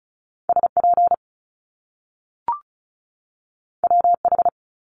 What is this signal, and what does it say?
Morse code, heard as a keyed tone, about 36 wpm, 700 Hz. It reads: SP WH
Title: Top Two Letter Combinations in Words 35wpm